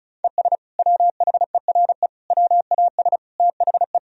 Morse code, heard as a keyed tone, about 35 wpm, 700 Hz. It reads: ES WHERE WAS THE
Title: Sentences from Top 200 Words and Abbreviations 35wpm